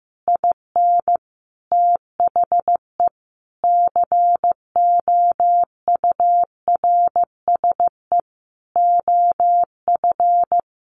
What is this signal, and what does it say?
Morse code, heard as a keyed tone, about 15 wpm, 700 Hz. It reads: IN THE COURSE OF
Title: Sentences from Top 1200 Words 15wpm